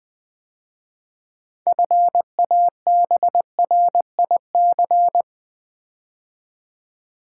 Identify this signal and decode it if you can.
Morse code, heard as a keyed tone, about 20 wpm, 700 Hz. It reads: FABRIC
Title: 6 Letter Words - First 3 Letters Given - 20wpm